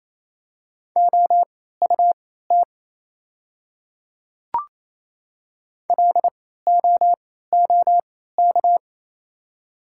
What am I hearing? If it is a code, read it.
Morse code, heard as a keyed tone, about 28 wpm, spaced as for 16 wpm, 700 Hz. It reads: OUT LOOK
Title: 3x Character Spacing - Top 100 Words - 28wpm